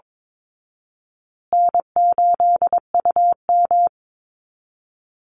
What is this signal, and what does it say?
Morse code, heard as a keyed tone, about 22 wpm, 700 Hz. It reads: N8UM
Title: Call Signs Repeated 2x - 22wpm